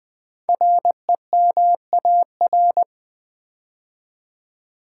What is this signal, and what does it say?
Morse code, heard as a keyed tone, about 20 wpm, 700 Hz. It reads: REMAR
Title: Build A Word - 5 Letters At A Time - 10-Letter Words - 20wpm